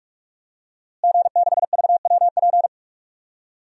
Morse code, heard as a keyed tone, about 43 wpm, 700 Hz. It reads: G6VWP
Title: DXCC-100 Call Signs - 5 Characters - Repeated 2x - 45wpm